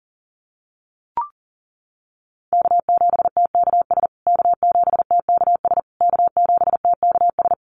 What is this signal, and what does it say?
Morse code, heard as a keyed tone, about 40 wpm, 700 Hz. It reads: K7TKS K7TKS K7TKS
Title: USA Call Signs Repeated 3x 40wpm